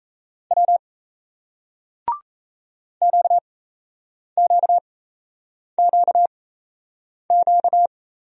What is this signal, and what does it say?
Morse code, heard as a keyed tone, about 36 wpm, spaced as for 8 wpm, 700 Hz. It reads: W QQQQ